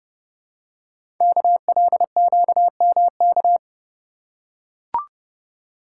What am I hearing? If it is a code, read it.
Morse code, heard as a keyed tone, about 30 wpm, 700 Hz. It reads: KLQMK
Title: Code Groups - 5 Letters - 30wpm